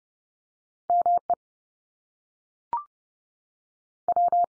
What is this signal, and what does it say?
Morse code, heard as a keyed tone, about 30 wpm, 700 Hz. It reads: ME W
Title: US States 30WPM